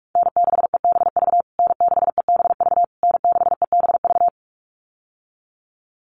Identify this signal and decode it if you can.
Morse code, heard as a keyed tone, about 44 wpm, 690 Hz. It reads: N6EBV N6EBV N6EBV
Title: USA Call Signs Repeated 3x 45wpm